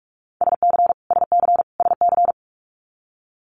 Morse code, heard as a keyed tone, about 46 wpm, 700 Hz. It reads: SC SC SC
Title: Multi-letter Phonemes Repeated 3x 45wpm